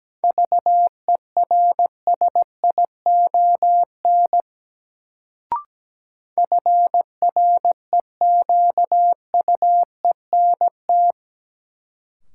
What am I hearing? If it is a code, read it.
Morse code, heard as a keyed tone, about 17 wpm, 700 Hz. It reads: VERSION FREQUENT
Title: Sentences from Top 1400 Words - New Word Drill - 17wpm